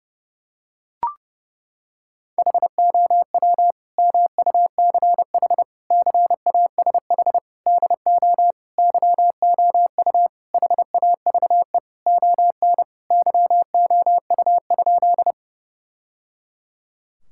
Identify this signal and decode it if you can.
Morse code, heard as a keyed tone, about 30 wpm, 700 Hz. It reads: HOW MUCH CASH DO YOU HAVE ON YOU?
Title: Sentences from Top 1000 Words - Review Part 5 - 30wpm